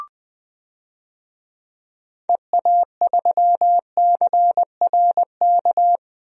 Morse code, heard as a keyed tone, about 21 wpm, 700 Hz. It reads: EA3CRK
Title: DXCC-100 Call Signs - 6 Characters - Repeated 2x - 20wpm